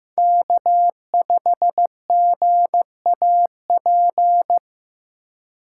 Morse code, heard as a keyed tone, about 15 wpm, 700 Hz. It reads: K5GAP
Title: USA Call Signs - 5 Characters - 4x Character Spacing - 15wpm